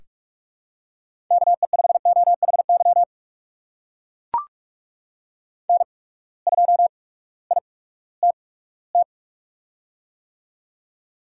Morse code, heard as a keyed tone, about 44 wpm, spaced as for 20 wpm, 700 Hz. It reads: KE5OHY N2ITT
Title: USA Call Signs - 8x Character Spacing - 45wpm